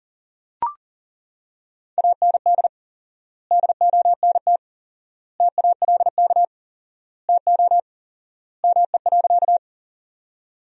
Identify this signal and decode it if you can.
Morse code, heard as a keyed tone, about 40 wpm, 700 Hz. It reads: AND DONT TALK TO ME.
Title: 4x Word-Spacing - Sentences from Top 400 Words - 40wpm